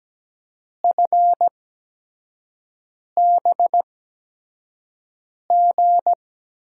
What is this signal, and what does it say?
Morse code, heard as a keyed tone, about 17 wpm, spaced as for 5 wpm, 700 Hz. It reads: FBG
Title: USA Call Signs - 8x Character Spacing - 17wpm